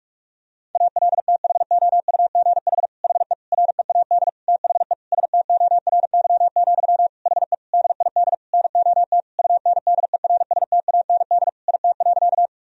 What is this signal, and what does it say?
Morse code, heard as a keyed tone, about 45 wpm, 700 Hz. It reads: ALTHOUGH HE READ THE STORY, HE DID NOT UNDERSTAND IT.
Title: Sentences from Top 1000 Words - Review Part 3 - 45wpm